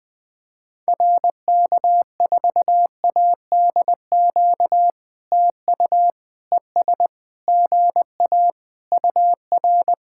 Morse code, heard as a keyed tone, about 20 wpm, 700 Hz. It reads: RK4ADQ TU ES GA UR
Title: POTA QSO Practice - 20wpm